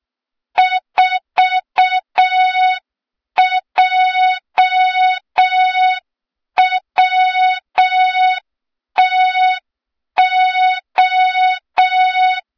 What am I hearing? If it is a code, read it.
Morse code, heard as a keyed tone, about 6 wpm, 3700 Hz. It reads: 4JWTO
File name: Morze-HA2MN / hang / 23 GYAKORLAT